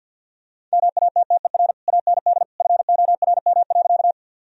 Morse code, heard as a keyed tone, about 52 wpm, 700 Hz. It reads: MATTER AND FORM.
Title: Sentences from Top 1000 Words - Review Part 1 - 50wpm